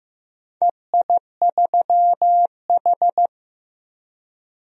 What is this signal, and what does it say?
Morse code, heard as a keyed tone, about 16 wpm, 700 Hz. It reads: EI3H
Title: DXCC-100 Call Signs - 4 Characters - 15wpm